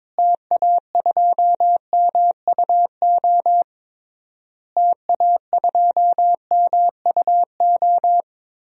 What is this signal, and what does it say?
Morse code, heard as a keyed tone, about 22 wpm, 700 Hz. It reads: TA2MUO TA2MUO
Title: DXCC-100 Call Signs - 6 Characters - Repeated 3x - 22wpm